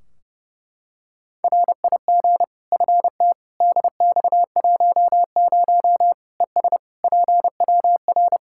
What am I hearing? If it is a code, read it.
Morse code, heard as a keyed tone, about 32 wpm, 700 Hz. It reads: RIG FT DX10 ES PWR
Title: Practice QSO - Exchange 2 - Info - 30wpm